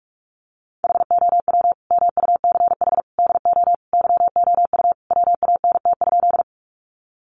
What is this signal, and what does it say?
Morse code, heard as a keyed tone, about 44 wpm, 700 Hz. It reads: HOW MUCH DO YOU WANT?
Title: Sentences from Top 200 Words 45wpm